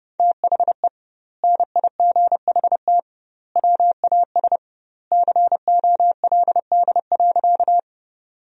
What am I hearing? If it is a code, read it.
Morse code, heard as a keyed tone, about 30 wpm, 700 Hz. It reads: THE NIGHT WAS COLD.
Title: 2x Word-Spacing - Sentences from Top 400 Words - 30wpm